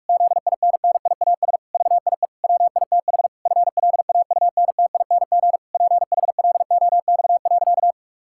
Morse code, heard as a keyed tone, about 46 wpm, 700 Hz. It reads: ZINNIAS VIE WITH FLAUNTING PHLOX.
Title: Pangram Challenge v2 - 45wpm